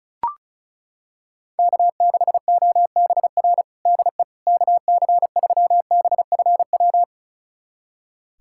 Morse code, heard as a keyed tone, about 34 wpm, 700 Hz. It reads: K6OBR DE KC3BFW
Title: USA Callsign DE Callsign 35wpm